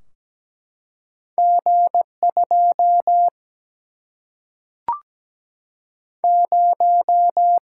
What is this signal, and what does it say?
Morse code, heard as a keyed tone, about 17 wpm, 700 Hz. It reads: G2 0
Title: Single-Letter Single-Number 17wpm